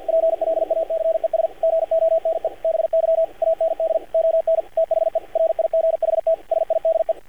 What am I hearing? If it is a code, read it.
Morse code, heard as a keyed tone, about 48 wpm, 590 Hz, with radio noise. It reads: PLACES GONE BY AND ON THE RIGHT SIDE
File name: archived / CRSnet